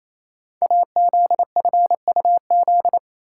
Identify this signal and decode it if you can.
Morse code, heard as a keyed tone, about 27 wpm, 700 Hz. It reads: AZFUZ
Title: Code Groups - 5 Letters - 28wpm